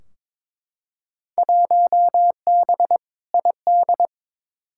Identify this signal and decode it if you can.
Morse code, heard as a keyed tone, about 22 wpm, 700 Hz. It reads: 1B ID
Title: Field Day Exchanges - 22wpm